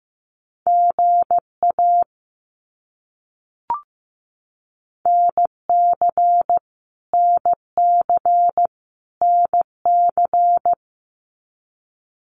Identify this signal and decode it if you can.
Morse code, heard as a keyed tone, about 15 wpm, 700 Hz. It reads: GA NC NC NC
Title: US States Repeated 3x 15wpm